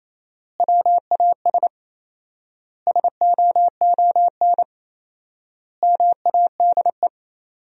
Morse code, heard as a keyed tone, about 28 wpm, 700 Hz. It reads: WAS SOON MADE
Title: 4x Word-Spacing - Sentences from Top 500 Words - 28wpm